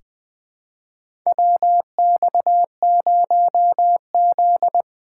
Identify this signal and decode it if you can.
Morse code, heard as a keyed tone, about 20 wpm, 700 Hz. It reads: WX0Z
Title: K1USN SST QSO Practice - 20wpm